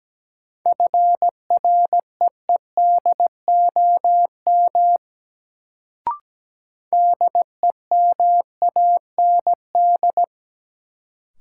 Morse code, heard as a keyed tone, about 17 wpm, 700 Hz. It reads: FREEDOM DEMAND
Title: Sentences from Top 800 Words - New Word Drill - 17wpm